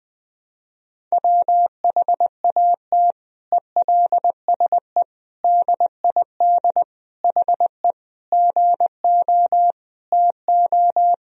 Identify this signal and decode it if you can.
Morse code, heard as a keyed tone, about 20 wpm, 700 Hz. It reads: WHAT ELSE DID HE GO TO
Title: Sentences from Top 1000 Words - Review Part 1 - 20wpm